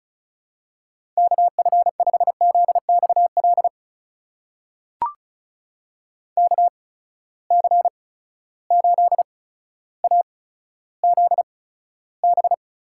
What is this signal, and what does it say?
Morse code, heard as a keyed tone, about 36 wpm, spaced as for 15 wpm, 700 Hz. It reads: KF5ZXL KC8AZB
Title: USA Call Signs - 6 Characters - 8x Character Spacing - 35wpm